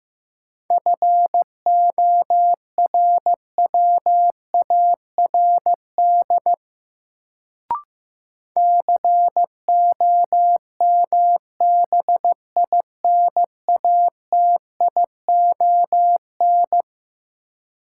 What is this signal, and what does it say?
Morse code, heard as a keyed tone, about 15 wpm, 700 Hz. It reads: FORWARD COMBINATION
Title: Sentences from Top 1300 Words - New Word Drill Alt - 15wpm